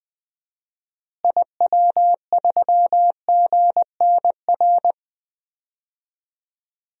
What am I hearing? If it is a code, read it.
Morse code, heard as a keyed tone, about 20 wpm, 700 Hz. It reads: IW3GNR
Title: DXCC-100 Call Signs - 6 Characters - Repeated 2x - 20wpm